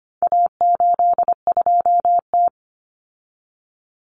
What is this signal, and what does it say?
Morse code, heard as a keyed tone, about 25 wpm, 700 Hz. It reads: A82T
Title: Common Cut Numbers - 4-Digit 25wpm